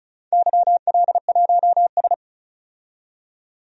Morse code, heard as a keyed tone, about 36 wpm, 700 Hz. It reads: YL1S
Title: DXCC-100 Call Signs - 4 Characters - 35wpm